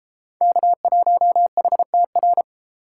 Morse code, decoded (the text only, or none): K1HTR